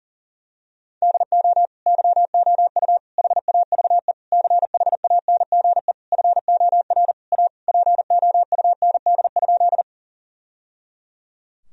Morse code, heard as a keyed tone, about 40 wpm, 700 Hz. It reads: DO YOU HAVE CHANGE FOR A POUND?
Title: Sentences from Top 1000 Words - Review Part 2 - 40wpm